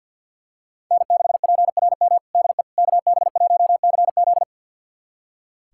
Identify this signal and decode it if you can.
Morse code, heard as a keyed tone, about 46 wpm, 690 Hz. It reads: N6PRM DE KB1CZ
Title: USA Callsign DE Callsign - 5 Characters - 50wpm